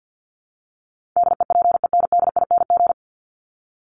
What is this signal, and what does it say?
Morse code, heard as a keyed tone, about 50 wpm, 700 Hz. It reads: DEPENDING
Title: ING Suffix - Top 100 Words with Base Words 50wpm